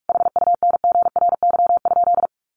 Morse code, heard as a keyed tone, about 44 wpm, 700 Hz. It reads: HUNGRY?
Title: Sentences from Top 1400 Words 45wpm